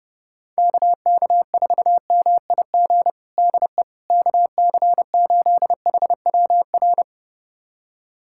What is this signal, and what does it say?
Morse code, heard as a keyed tone, about 30 wpm, 700 Hz. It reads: KK4MIG DE KC8HWR
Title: USA Callsign DE Callsign 30wpm